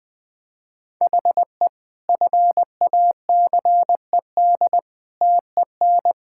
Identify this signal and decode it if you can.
Morse code, heard as a keyed tone, about 20 wpm, 700 Hz. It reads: HE FACED TEN
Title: Sentences with 9-Word Phrases - 20wpm